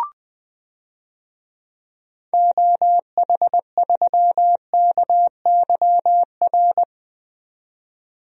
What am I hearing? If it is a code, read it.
Morse code, heard as a keyed tone, about 20 wpm, 700 Hz. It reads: OH3KYR
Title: DXCC-100 Call Signs - 6 Characters - 20wpm